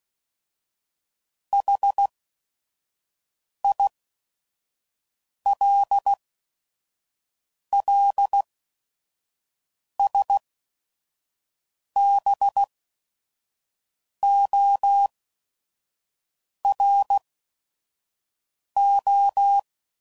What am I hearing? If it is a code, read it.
Morse code, heard as a keyed tone, about 16 wpm, spaced as for 5 wpm, 800 Hz. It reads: HILLSBORO
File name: CW Code Groups @5WPM / fromSM's / Cities